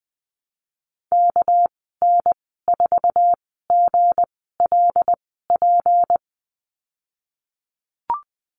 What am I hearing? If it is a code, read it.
Morse code, heard as a keyed tone, about 20 wpm, spaced as for 14 wpm, 700 Hz. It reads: KN4GLP
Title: 245 Lesson 39g - Letter z - Call Signs 20wpm